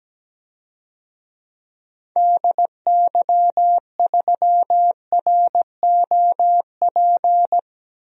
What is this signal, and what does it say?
Morse code, heard as a keyed tone, about 17 wpm, 700 Hz. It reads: DY3ROP
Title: DXCC-100 Call Signs - 6 Characters - Repeated 2x - 17wpm